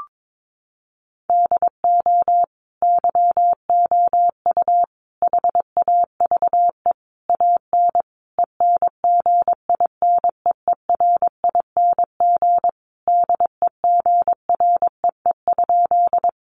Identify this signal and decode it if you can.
Morse code, heard as a keyed tone, about 22 wpm, 700 Hz. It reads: DO YOU HAVE AN ENGINEERING DEGREE?
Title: Sentences from Top 1500 Words 22wpm